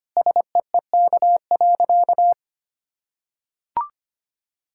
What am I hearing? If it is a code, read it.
Morse code, heard as a keyed tone, about 25 wpm, 700 Hz. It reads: SEEK.
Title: Sentences from Top 1600 Words 25wpm